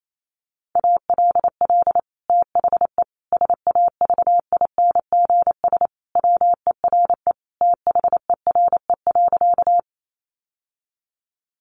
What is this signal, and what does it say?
Morse code, heard as a keyed tone, about 28 wpm, 700 Hz. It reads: ALL THE SAVINGS WERE THERE.
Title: Sentences from Top 1200 Words 28wpm